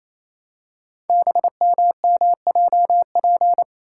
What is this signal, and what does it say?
Morse code, heard as a keyed tone, about 28 wpm, 700 Hz. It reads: BMMJP
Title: Code Groups - 5 Letter:Numbers - 28wpm